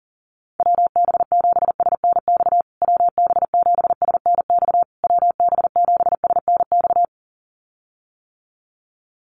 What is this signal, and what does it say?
Morse code, heard as a keyed tone, about 40 wpm, 700 Hz. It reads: WB7SNX WB7SNX WB7SNX
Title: USA Call Signs Repeated 3x 40wpm